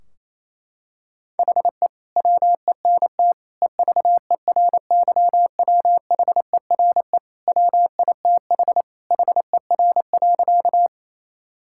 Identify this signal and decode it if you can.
Morse code, heard as a keyed tone, about 28 wpm, 700 Hz. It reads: HE WENT EVERYWHERE WITH HER.
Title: Sentences from Top 1600 Words 28wpm